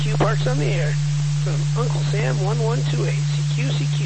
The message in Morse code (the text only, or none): none